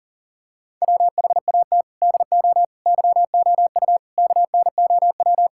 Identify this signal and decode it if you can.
Morse code, heard as a keyed tone, about 37 wpm, 700 Hz. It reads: WHAT DO YOU KNOW
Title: Sentences from Top 1000 Words - Review Part 5 - 40wpm